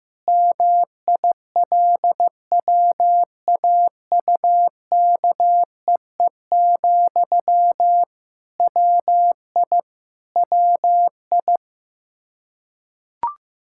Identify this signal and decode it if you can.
Morse code, heard as a keyed tone, about 15 wpm, 700 Hz. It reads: MILWAUKEE, WI WI
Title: K1USN SST QSO Practice 15wpm